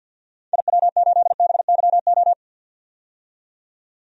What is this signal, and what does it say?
Morse code, heard as a keyed tone, about 48 wpm, 700 Hz. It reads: IW8BYO
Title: DXCC-100 Call Signs - 6 Characters - 50wpm